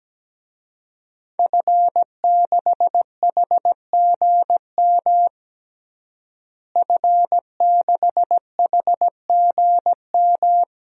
Morse code, heard as a keyed tone, about 17 wpm, 700 Hz. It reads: F6HGM F6HGM
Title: DXCC-100 Call Signs - 5 Characters - Repeated 2x - 17wpm